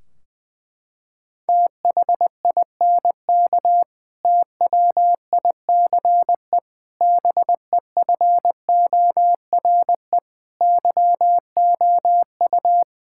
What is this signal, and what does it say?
Morse code, heard as a keyed tone, about 21 wpm, 700 Hz. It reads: THINK TWICE BEFORE YOU
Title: Sentences from Top 1600 Words 20wpm